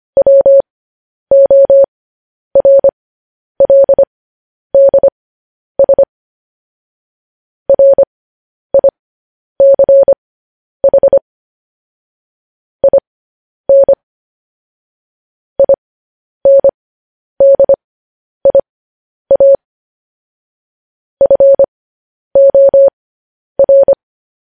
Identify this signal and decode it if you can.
Morse code, heard as a keyed tone, about 25 wpm, spaced as for 10 wpm, 550 Hz. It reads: WORLDS RICH IN INDIA FOR
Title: News Headlines at 10